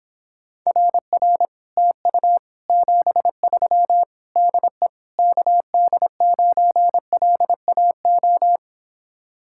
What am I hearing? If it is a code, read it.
Morse code, heard as a keyed tone, about 26 wpm, 700 Hz. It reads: RR TU 73 DE KD9LAO